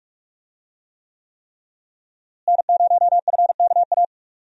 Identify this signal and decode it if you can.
Morse code, heard as a keyed tone, about 47 wpm, 700 Hz. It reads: N0FKA
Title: USA Call Signs - 5 Characters - Repeated 3x - 45wpm